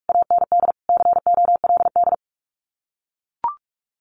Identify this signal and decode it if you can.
Morse code, heard as a keyed tone, about 44 wpm, 700 Hz. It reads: AND COLD
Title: Common QSO Phrases Encore 45wpm